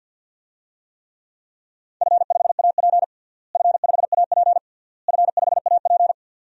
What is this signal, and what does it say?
Morse code, heard as a keyed tone, about 51 wpm, 700 Hz. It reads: F5AP F5AP F5AP
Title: DXCC-100 Call Signs - 4 Characters - Repeated 3x - 50wpm